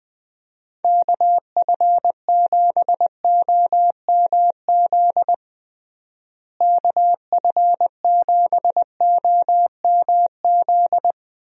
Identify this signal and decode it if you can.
Morse code, heard as a keyed tone, about 21 wpm, 700 Hz. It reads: KF7OMZ KF7OMZ
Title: USA Call Signs - 6 Characters - Repeated 2x - 20wpm